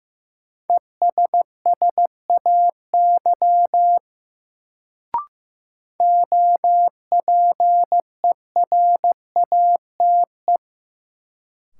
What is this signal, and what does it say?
Morse code, heard as a keyed tone, about 15 wpm, 700 Hz. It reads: ESSAY OPERATE
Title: Sentences from Top 1500 Words - New Word Drill Alt - 15wpm